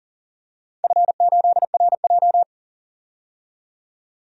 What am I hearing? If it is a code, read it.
Morse code, heard as a keyed tone, about 40 wpm, 700 Hz. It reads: F8RJ
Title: DXCC-100 Call Signs - 4 Characters - 40wpm